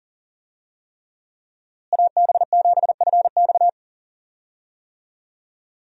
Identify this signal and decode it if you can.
Morse code, heard as a keyed tone, about 40 wpm, 700 Hz. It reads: AB7FX
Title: USA Call Signs - 5 Characters - Repeated 3x - 40wpm